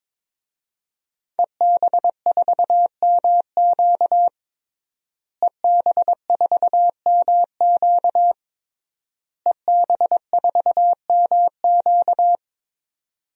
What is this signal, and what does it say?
Morse code, heard as a keyed tone, about 22 wpm, 700 Hz. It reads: EB4MQ EB4MQ EB4MQ
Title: DXCC-100 Call Signs - 5 Characters - Repeated 3x - 22wpm